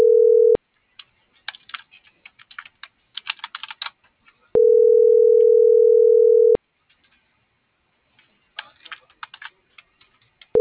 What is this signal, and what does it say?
Ringback tone (North America), 440 + 480 Hz.